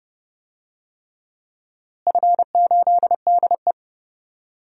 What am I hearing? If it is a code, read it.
Morse code, heard as a keyed tone, about 30 wpm, 700 Hz. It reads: F8DE
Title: DXCC-100 Call Signs - 4 Characters - Repeated 3x - 30wpm